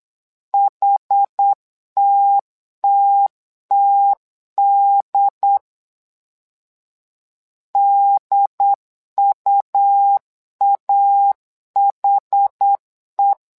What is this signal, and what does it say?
Morse code, heard as a keyed tone, about 8 wpm, 800 Hz. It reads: HTTTD DUAHE
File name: LEZIONI / 2 UHDN / 2 UHD 8wpm 2258